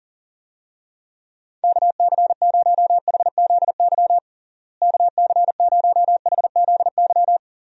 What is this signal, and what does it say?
Morse code, heard as a keyed tone, about 40 wpm, 700 Hz. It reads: KC0HZY KC0HZY
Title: USA Call Signs - 6 Characters - Repeated 2x - 40wpm